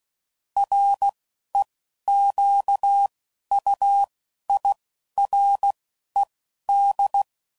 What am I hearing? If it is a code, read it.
Morse code, heard as a keyed tone, about 16 wpm, spaced as for 11 wpm, 800 Hz. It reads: REQUIRED
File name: CW Code Groups @10WPM / Technical Subjects / SwPwrSup